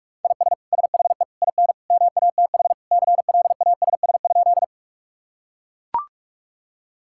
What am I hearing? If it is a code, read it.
Morse code, heard as a keyed tone, about 42 wpm, 700 Hz. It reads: IS SHE IN MATH CLASS?
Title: Sentences from Top 1000 Words - Review Part 6 - 45wpm